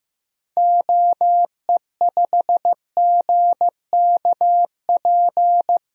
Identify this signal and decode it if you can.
Morse code, heard as a keyed tone, about 15 wpm, 700 Hz. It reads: OE5GKP
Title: DXCC-100 Call Signs - 6 Characters - 15wpm